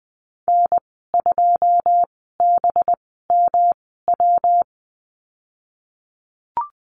Morse code, heard as a keyed tone, about 20 wpm, spaced as for 15 wpm, 700 Hz. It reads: N2BMW
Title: 207 Lesson 33g - Letter j - Call Signs 20wpm